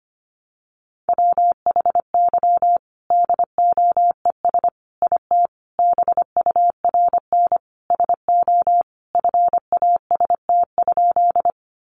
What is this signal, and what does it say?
Morse code, heard as a keyed tone, about 25 wpm, 700 Hz. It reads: WHY DOES IT BURN SO FAST?
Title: Sentences from Top 1300 Words 25wpm